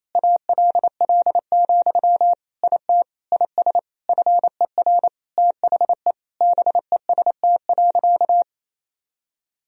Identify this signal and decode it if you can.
Morse code, heard as a keyed tone, about 27 wpm, 700 Hz. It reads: ALL, IT IS FER THE BEST.
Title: Sentences from Top 500 Words and Abbreviations 28wpm